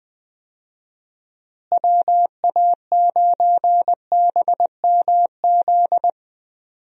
Morse code, heard as a keyed tone, about 20 wpm, 700 Hz. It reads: WA9BMZ
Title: USA Call Signs - 6 Characters - Repeated 2x - 20wpm